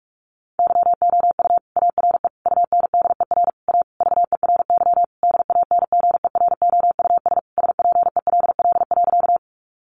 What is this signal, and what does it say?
Morse code, heard as a keyed tone, about 44 wpm, 700 Hz. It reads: YOU ARE UNDER A VERY DANGEROUS SPELL.
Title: Sentences from Top 1200 Words 45wpm